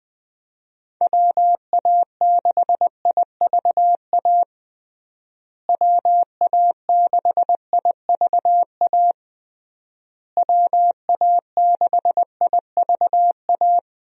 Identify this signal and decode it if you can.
Morse code, heard as a keyed tone, about 20 wpm, 700 Hz. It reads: WA6IVA WA6IVA WA6IVA
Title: USA Call Signs - 6 Characters - Repeated 3x - 20wpm